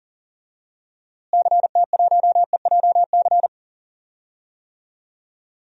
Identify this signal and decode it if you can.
Morse code, heard as a keyed tone, about 40 wpm, 700 Hz. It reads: CT1EJC
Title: DXCC-100 Call Signs - 6 Characters - Repeated 2x - 40wpm